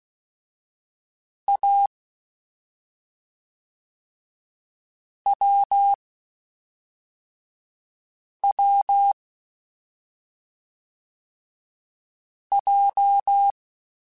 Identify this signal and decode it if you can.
Morse code, heard as a keyed tone, about 16 wpm, 800 Hz. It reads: A W W J
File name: Short Runs / Short Runs-mp3s / awj1ndb6gz7 @5